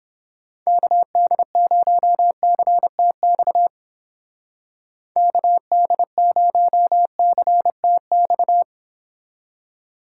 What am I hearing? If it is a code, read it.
Morse code, heard as a keyed tone, about 30 wpm, 700 Hz. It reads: KD0CTX KD0CTX